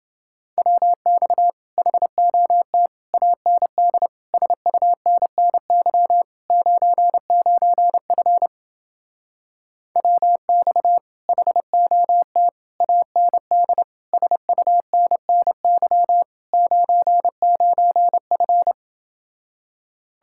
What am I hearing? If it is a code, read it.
Morse code, heard as a keyed tone, about 28 wpm, 700 Hz. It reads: WX HOT AND SUNNY 99F WX HOT AND SUNNY 99F